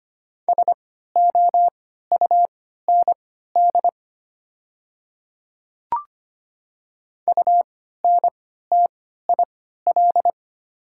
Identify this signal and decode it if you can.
Morse code, heard as a keyed tone, about 25 wpm, spaced as for 14 wpm, 700 Hz. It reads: SOUND UNTIL
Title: 3x Character Spacing - Top 300 Words - 25wpm